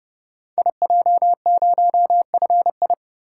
Morse code, heard as a keyed tone, about 30 wpm, 700 Hz. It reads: IJ0FI
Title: Code Groups - 5 Letter:Numbers - 30wpm